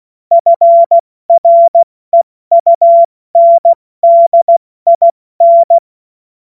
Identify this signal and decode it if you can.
Morse code, heard as a keyed tone, about 16 wpm, spaced as for 14 wpm, 680 Hz. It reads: FREUNDIN
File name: Morsekurs / Witze / Witz 05 März 2024 70 F80